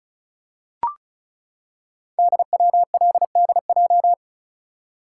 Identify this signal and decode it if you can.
Morse code, heard as a keyed tone, about 36 wpm, 700 Hz. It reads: DWLDJ
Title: Code Groups - 5 Letters - 35wpm